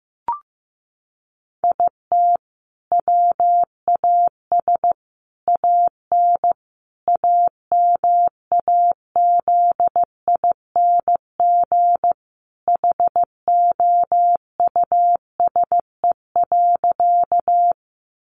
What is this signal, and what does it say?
Morse code, heard as a keyed tone, about 14 wpm, 700 Hz. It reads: IT WAS AN AMAZING HOUSE.
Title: Sentences from Top 1500 Words 15wpm